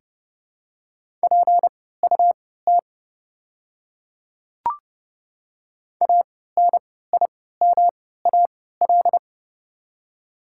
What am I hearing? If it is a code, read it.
Morse code, heard as a keyed tone, about 30 wpm, spaced as for 17 wpm, 690 Hz. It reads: PUT ANIMAL
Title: 3x Character Spacing - Top 200 Words - 30wpm